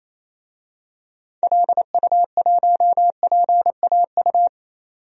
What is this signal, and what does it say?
Morse code, heard as a keyed tone, about 27 wpm, 700 Hz. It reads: LU1PAU
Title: DXCC-100 Call Signs - 6 Characters - 28wpm